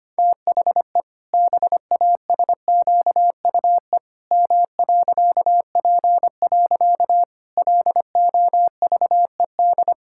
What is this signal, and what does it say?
Morse code, heard as a keyed tone, about 25 wpm, 700 Hz. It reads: THE BASQUE M.P. LOVED
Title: Pangram Challenge v2 - 25wpm